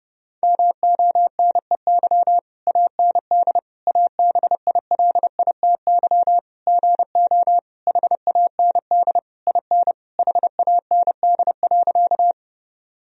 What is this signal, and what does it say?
Morse code, heard as a keyed tone, about 30 wpm, 700 Hz. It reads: MONEY AND ABILITY GO HAND IN HAND.
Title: Sentences from Top 1000 Words - Review Part 3 - 30wpm